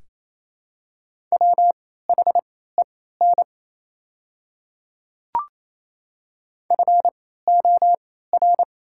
Morse code, heard as a keyed tone, about 28 wpm, spaced as for 16 wpm, 690 Hz. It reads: WHEN FOR
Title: 3x Character Spacing - Top 100 Words - 28wpm